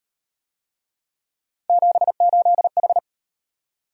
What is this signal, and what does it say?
Morse code, heard as a keyed tone, about 38 wpm, 700 Hz. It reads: 78H